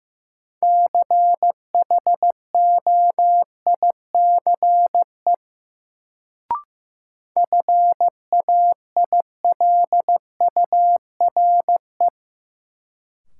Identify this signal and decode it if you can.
Morse code, heard as a keyed tone, about 15 wpm, 700 Hz. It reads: CHOICE FAILURE
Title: Sentences from Top 900 Words - New Word Drill - 15wpm